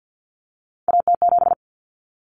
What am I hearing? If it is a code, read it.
Morse code, heard as a keyed tone, about 50 wpm, 700 Hz. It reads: AT7
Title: Common Cut Numbers - 3-Digit 50wpm